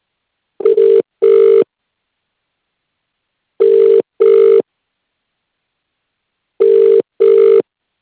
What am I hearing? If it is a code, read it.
Ringback tone (UK), 400 + 450 Hz.